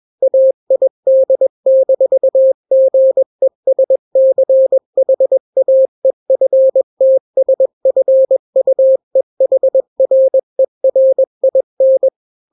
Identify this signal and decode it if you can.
Morse code, heard as a keyed tone, about 21 wpm, 530 Hz. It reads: AID-GESCHAEFTSFUEHRERIN
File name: daily / 20wpm